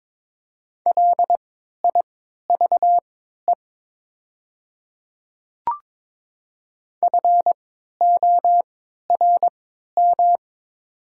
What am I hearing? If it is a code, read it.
Morse code, heard as a keyed tone, about 22 wpm, spaced as for 13 wpm, 700 Hz. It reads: LIVE FORM
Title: 3x Character Spacing - Top 500 Words - 22wpm